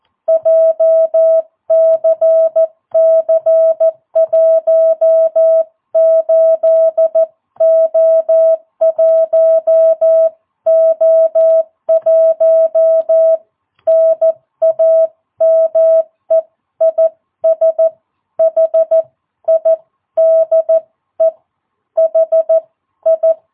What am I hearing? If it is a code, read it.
Morse code, heard as a keyed tone, about 14 wpm, spaced as for 12 wpm, 660 Hz. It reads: JCC18O1O1 NAME IS HIDE HI
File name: jh8cbh / 2022cw